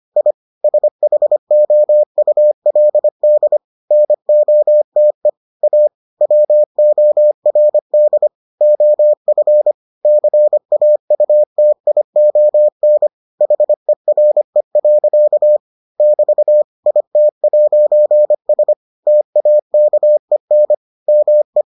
Morse code, heard as a keyed tone, about 25 wpm, 600 Hz. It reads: I SHOULD NOTE A WORD OF CAUTION HERE. <BT> IT'S TAKEN ME